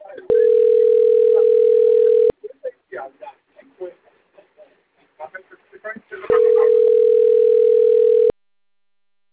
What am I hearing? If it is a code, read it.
Ringback tone (North America), 440 + 480 Hz, with speech.
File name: rg-1600-unknown- - - .11104